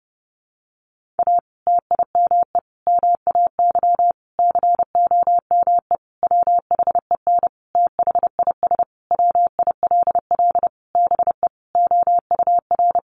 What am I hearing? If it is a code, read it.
Morse code, heard as a keyed tone, about 30 wpm, 700 Hz. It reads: A TIME MAY COME WHEN THIS WILL BE OUR
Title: Sentences from Top 1500 Words 30wpm